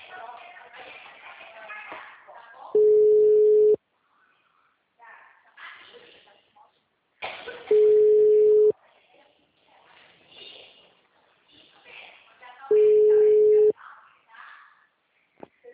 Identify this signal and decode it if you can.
Ringback tone (Europe, 425 Hz), with speech.
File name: out- -2015- - - .132